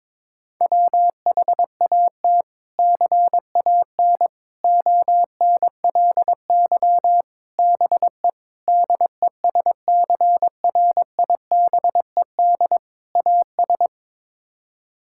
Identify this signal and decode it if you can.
Morse code, heard as a keyed tone, about 21 wpm, 700 Hz. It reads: WHAT CAN ONLY BE DESCRIBED AS
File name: practice / 22wpm / Sets of 6 Words - 22wpm